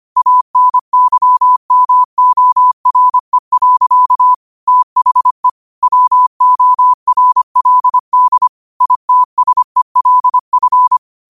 Morse code, heard as a keyed tone, about 25 wpm, 1000 Hz. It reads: ANYMORE. THE WORLD ITSELF